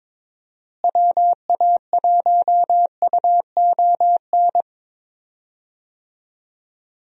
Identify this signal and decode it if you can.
Morse code, heard as a keyed tone, about 22 wpm, 700 Hz. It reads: WA1UON
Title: USA Call Signs - 6 Characters - Repeated 2x - 22wpm